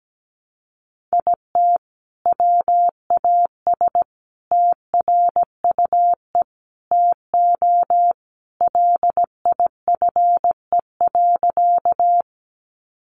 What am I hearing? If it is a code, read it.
Morse code, heard as a keyed tone, about 17 wpm, 700 Hz. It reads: IT WAS TRUE TO LIFE.
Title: Sentences from Top 400 Words 17wpm